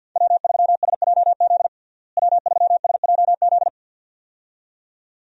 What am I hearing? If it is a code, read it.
Morse code, heard as a keyed tone, about 48 wpm, 700 Hz. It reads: W3SJZ W3SJZ
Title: USA Call Signs - 5 Characters - Repeated 2x - 50wpm